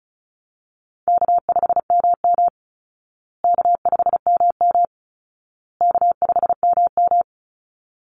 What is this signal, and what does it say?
Morse code, heard as a keyed tone, about 34 wpm, 700 Hz. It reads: K5MM K5MM K5MM
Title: Easy Call Signs Repeated 3x - 35wpm